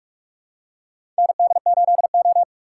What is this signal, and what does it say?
Morse code, heard as a keyed tone, about 44 wpm, 700 Hz. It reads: ND8O
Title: USA Call Signs - 4 Characters - 2x Character Spacing - 45wpm